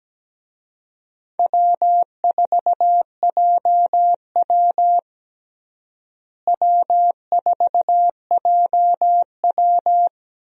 Morse code, heard as a keyed tone, about 17 wpm, 700 Hz. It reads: W4JW W4JW
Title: USA Call Signs - 4 Characters - Repeated 3x - 17wpm